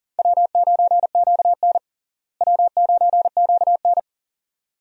Morse code, heard as a keyed tone, about 40 wpm, 700 Hz. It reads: W9QN W9QN
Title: USA Call Signs - 4 Characters - Repeated 2x - 40wpm